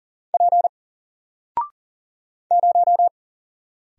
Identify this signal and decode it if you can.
Morse code, heard as a keyed tone, about 38 wpm, 700 Hz. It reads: P 0
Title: Single Character (Letter:Number:Punctuation:Prosign) 40wpm